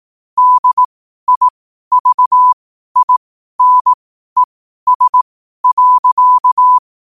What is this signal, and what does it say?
Morse code, heard as a keyed tone, about 18 wpm, spaced as for 12 wpm, 1000 Hz. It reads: DIVINES.
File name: qotdfiles / 12wpm